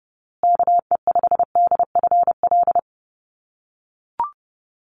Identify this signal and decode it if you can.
Morse code, heard as a keyed tone, about 30 wpm, 700 Hz. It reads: KE5DFL
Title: USA Call Signs Repeated 3x 30wpm